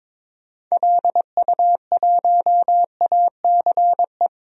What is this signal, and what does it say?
Morse code, heard as a keyed tone, about 22 wpm, 700 Hz. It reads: LU1ACE
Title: DXCC-100 Call Signs - 6 Characters - 22wpm